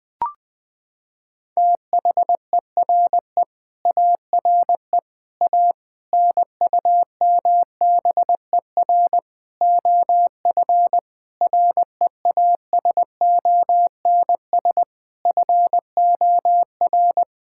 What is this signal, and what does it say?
Morse code, heard as a keyed tone, about 21 wpm, 700 Hz. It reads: THERE ARE A NUMBER OF REASONS FOR
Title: Sentences with 5-Word Phrases - 20wpm